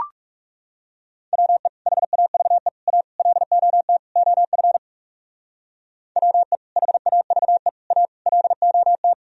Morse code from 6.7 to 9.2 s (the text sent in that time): HAVE A LOT